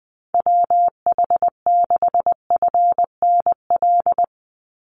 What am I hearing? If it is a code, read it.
Morse code, heard as a keyed tone, about 20 wpm, 700 Hz. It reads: WH6FNL
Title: USA Call Signs 20wpm